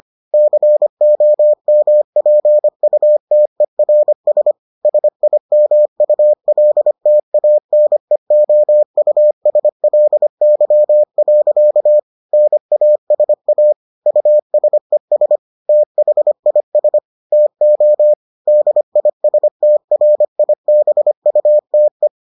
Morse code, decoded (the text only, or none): COMPUTERS SIMULTANEOUSLY. NASA USES THIS TO DISTRIBUTE